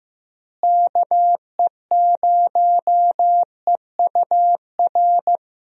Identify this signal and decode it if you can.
Morse code, heard as a keyed tone, about 15 wpm, 700 Hz. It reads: KE0EUR
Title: USA Call Signs - 6 Characters - Repeated 2x - 15wpm